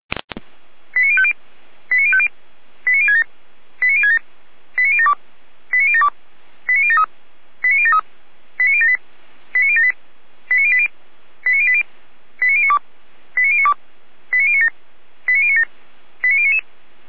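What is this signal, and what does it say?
Five-tone selective call (ZVEI), heard as quick runs of five tones, tones of about 70 ms. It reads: 89050 89050 89067 89067 89072 89072 89073 89073 89078 89078 89080 89080 89092 89092 89097 89097 89099
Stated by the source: Five-tone ZVEI selective call received by SDR.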